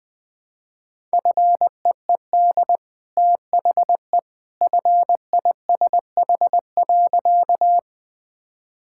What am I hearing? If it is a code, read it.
Morse code, heard as a keyed tone, about 20 wpm, 700 Hz. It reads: FEED THE FISH.
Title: Easy Sentences with Hints - 20wpm